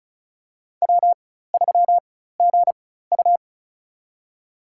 Morse code, heard as a keyed tone, about 34 wpm, spaced as for 16 wpm, 700 Hz. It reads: W3GU
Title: USA Call Signs - 4 Characters - 4x Character Spacing - 35wpm